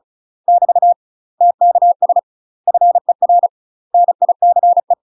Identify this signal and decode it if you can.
Morse code, heard as a keyed tone, about 36 wpm, 700 Hz. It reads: <BT> TKS FER NICE
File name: UK MASTER QSOs TRAINING / 35WPM / UK MASTER QSOs 0032